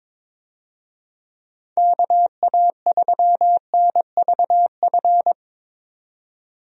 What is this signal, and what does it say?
Morse code, heard as a keyed tone, about 22 wpm, 700 Hz. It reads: KA3NVF
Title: USA Call Signs - 6 Characters - 22wpm